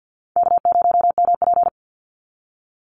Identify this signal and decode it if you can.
Morse code, heard as a keyed tone, about 48 wpm, 700 Hz. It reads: K0MP
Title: NA SOTA Chaser Callsigns 50wpm